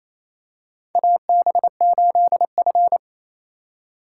Morse code, heard as a keyed tone, about 28 wpm, 700 Hz. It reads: AB8F
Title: USA Call Signs - 4 Characters - 8x Character Spacing - 28wpm